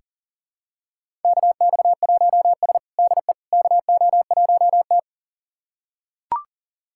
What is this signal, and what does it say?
Morse code, heard as a keyed tone, about 40 wpm, 700 Hz. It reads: KX1S DE KO1T
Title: USA Callsign DE Callsign - 4 Characters - 40wpm